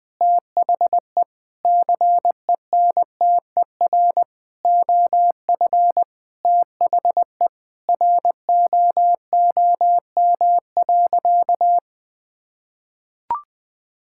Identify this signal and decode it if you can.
Morse code, heard as a keyed tone, about 21 wpm, 700 Hz. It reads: THE CENTER OF THE ROOM.
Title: Sentences with 4-Word Phrases - 20wpm